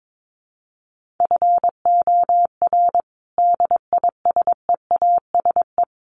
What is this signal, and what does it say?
Morse code, heard as a keyed tone, about 22 wpm, 700 Hz. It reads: FOR DISEASE
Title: Sets of 5 Words Encore2 22wpm